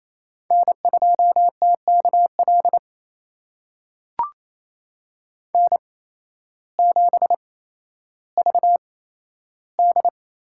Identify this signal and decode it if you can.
Morse code, heard as a keyed tone, about 28 wpm, spaced as for 13 wpm, 700 Hz. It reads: N2TKL N7VD
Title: USA Call Signs - 8x Character Spacing - 28wpm